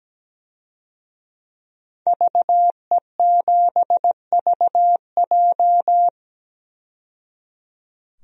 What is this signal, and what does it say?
Morse code, heard as a keyed tone, about 17 wpm, 700 Hz. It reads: VE7VJ
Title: DXCC-100 Call Signs - 5 Characters - 17wpm